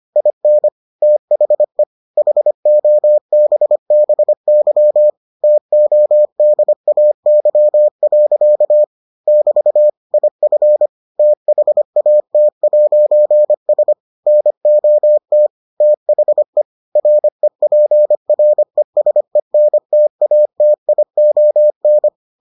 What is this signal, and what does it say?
Morse code, heard as a keyed tone, about 25 wpm, 600 Hz. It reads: IN THE HOBBY TODAY. <BT> IF THAT'S NOT THE REPRESENTATION